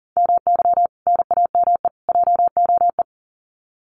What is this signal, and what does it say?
Morse code, heard as a keyed tone, about 40 wpm, 700 Hz. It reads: MY NAME JOE
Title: Common QSO Phrases 40wpm